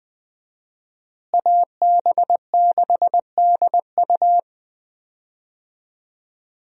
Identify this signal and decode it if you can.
Morse code, heard as a keyed tone, about 20 wpm, 700 Hz. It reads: AB6DU
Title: USA Call Signs - 5 Characters - Repeated 2x - 20wpm